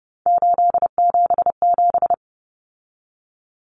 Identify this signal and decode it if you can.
Morse code, heard as a keyed tone, about 30 wpm, 700 Hz. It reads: 877
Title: Common Cut Numbers - 3-Digit 30wpm